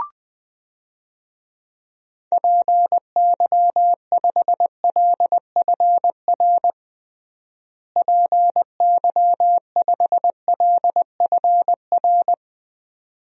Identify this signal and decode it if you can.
Morse code, heard as a keyed tone, about 21 wpm, 700 Hz. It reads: PY5LFR PY5LFR
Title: DXCC-100 Call Signs - 6 Characters - Repeated 2x - 20wpm